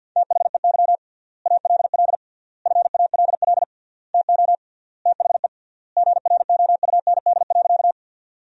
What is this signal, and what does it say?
Morse code, heard as a keyed tone, about 51 wpm, 700 Hz. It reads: THEY ALL FALL TO THE GROUND.
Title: 3x Word-Spacing - Sentences from Top 500 Words - 50wpm